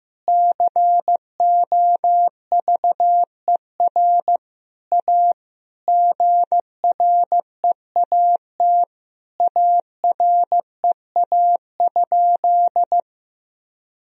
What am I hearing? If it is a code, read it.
Morse code, heard as a keyed tone, about 15 wpm, 700 Hz. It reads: COVER A GREAT AREA?
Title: Sentences from Top 1000 Words - Review Part 2 - 15wpm